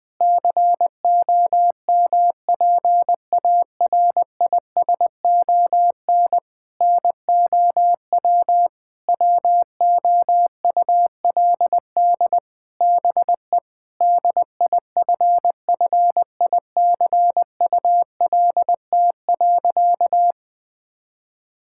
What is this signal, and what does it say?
Morse code, heard as a keyed tone, about 21 wpm, 700 Hz. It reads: COMPARISON NOW WOULD BE DIFFICULT.
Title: Sentences from Top 1600 Words 20wpm